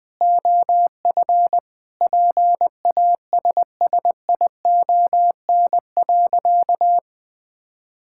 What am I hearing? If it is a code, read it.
Morse code, heard as a keyed tone, about 20 wpm, 700 Hz. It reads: OF PASSION.
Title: Sentences from Top 1600 Words 20wpm